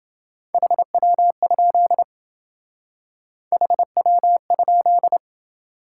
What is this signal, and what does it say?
Morse code, heard as a keyed tone, about 27 wpm, 700 Hz. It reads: HW? HW?